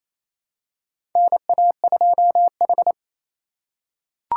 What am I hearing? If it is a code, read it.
Morse code, heard as a keyed tone, about 28 wpm, 700 Hz. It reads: NA2H
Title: USA Call Signs - 4 Characters - 3x Character Spacing - 28wpm